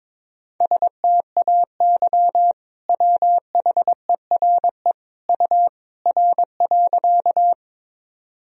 Morse code, heard as a keyed tone, about 22 wpm, 700 Hz. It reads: STAY WHERE U R.
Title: Sentences from Top 500 Words and Abbreviations 22wpm